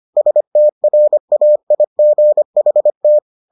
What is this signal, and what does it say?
Morse code, heard as a keyed tone, about 25 wpm, 600 Hz. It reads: STRAIGHT